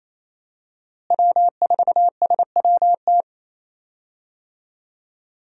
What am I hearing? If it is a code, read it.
Morse code, heard as a keyed tone, about 28 wpm, 700 Hz. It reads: W4SWT
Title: USA Call Signs - 5 Characters - Repeated 2x - 28wpm